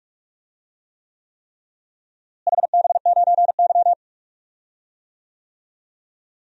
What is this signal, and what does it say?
Morse code, heard as a keyed tone, about 42 wpm, 700 Hz. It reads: HB9Y
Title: DXCC-100 Call Signs - 4 Characters - Repeated 3x - 45wpm